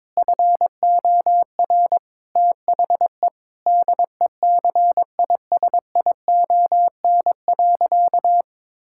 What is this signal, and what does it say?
Morse code, heard as a keyed tone, about 22 wpm, 700 Hz. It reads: FOR THE DECISION.
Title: Sentences from Top 1600 Words 22wpm